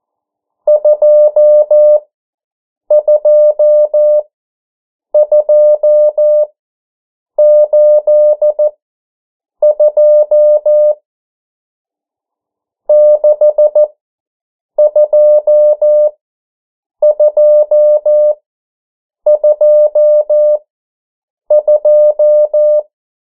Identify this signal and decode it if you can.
Morse code, heard as a keyed tone, about 14 wpm, spaced as for 7 wpm, 600 Hz. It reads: 22282 62222
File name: morse mp3 / 09 urok 8